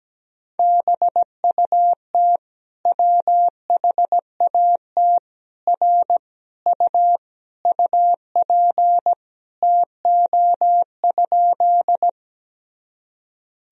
Morse code, heard as a keyed tone, about 17 wpm, 700 Hz. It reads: BUT WHAT R U UP TO?
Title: Sentences from Top 100 Words and Abbreviations 17wpm